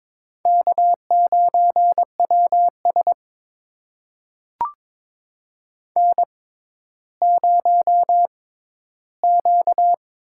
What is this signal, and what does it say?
Morse code, heard as a keyed tone, about 23 wpm, spaced as for 13 wpm, 700 Hz. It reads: K9WS N0Q
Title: USA Call Signs - 6x Character Spacing - 22wpm